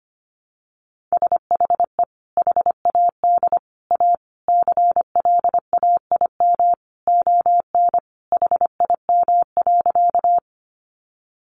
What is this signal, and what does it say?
Morse code, heard as a keyed tone, about 25 wpm, 700 Hz. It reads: SHE HAD A CLAIM ON HIM.
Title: Sentences from Top 1500 Words 25wpm